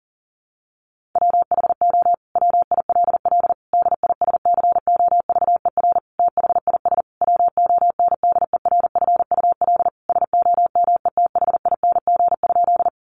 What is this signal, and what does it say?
Morse code, heard as a keyed tone, about 40 wpm, 700 Hz. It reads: WHO WILL DISCOVER THIS WONDERFUL SOMETHING?
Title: Sentences from Top 1100 Words 40wpm